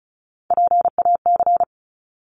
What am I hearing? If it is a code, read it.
Morse code, heard as a keyed tone, about 34 wpm, 700 Hz. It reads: PAC
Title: ARRL Sections 35wpm